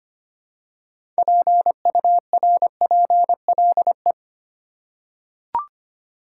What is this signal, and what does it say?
Morse code, heard as a keyed tone, about 25 wpm, 700 Hz. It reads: PURPLE
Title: Six Letter Words - 25wpm